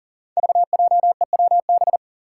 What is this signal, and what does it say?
Morse code, heard as a keyed tone, about 37 wpm, 700 Hz. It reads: VJEWB
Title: Five Challenging Letter-Number Combinations 40wpm